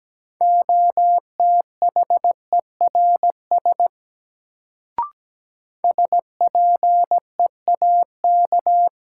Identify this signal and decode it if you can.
Morse code, heard as a keyed tone, about 17 wpm, 700 Hz. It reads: OTHERS SPEAK
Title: Sentences from Top 1000 Words - Review Part 2 - New Word Drill - 17wpm